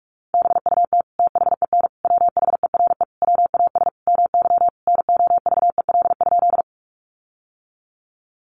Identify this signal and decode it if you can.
Morse code, heard as a keyed tone, about 46 wpm, 700 Hz. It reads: BUT THEN WHERE WAS MY NOVEL?
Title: Sentences from Top 1300 Words 45wpm